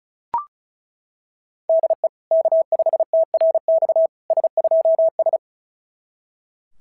Morse code, heard as a keyed tone, about 36 wpm, 660 Hz. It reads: DE K5TRX S2S
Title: SOTA QSO Practice 35wpm